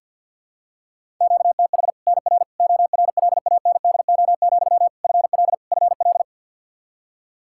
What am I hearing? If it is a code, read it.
Morse code, heard as a keyed tone, about 47 wpm, 700 Hz. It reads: QTH NR ORLANDO, FL FL
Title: Practice QSO - Exchange 1 - Beginning - 50wpm